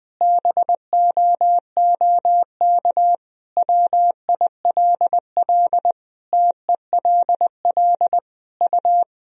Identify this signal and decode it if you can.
Morse code, heard as a keyed tone, about 21 wpm, 700 Hz. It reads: BOOK WILL TELL U
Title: Sentences from Top 300 Words and Abbreviations 20wpm